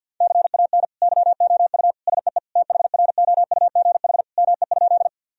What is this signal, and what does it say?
Morse code, heard as a keyed tone, about 48 wpm, 700 Hz. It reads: CAN YOU SEE THROUGH ME?
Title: Sentences from Top 1000 Words - Review Part 1 - 50wpm